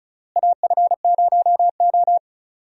Morse code, heard as a keyed tone, about 36 wpm, 700 Hz. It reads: AF0O
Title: USA Call Signs - 4 Characters - Repeated 3x - 35wpm